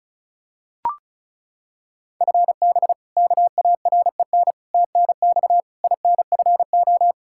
Morse code, heard as a keyed tone, about 36 wpm, 700 Hz. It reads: FB KAREN TNX INFO
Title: K1USN SST QSO Practice 35wpm